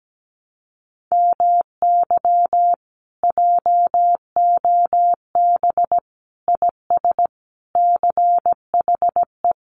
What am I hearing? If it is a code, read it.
Morse code, heard as a keyed tone, about 17 wpm, 700 Hz. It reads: MY JOB IS CHE
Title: Common QSO Phrases Encore 17wpm